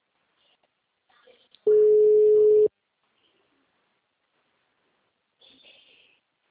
Ringback tone (Europe, 425 Hz).